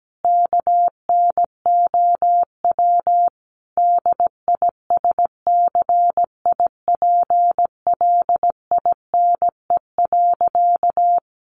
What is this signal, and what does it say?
Morse code, heard as a keyed tone, about 17 wpm, 700 Hz. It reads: KNOW DISCIPLINE.
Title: Sentences from Top 900 Words 17wpm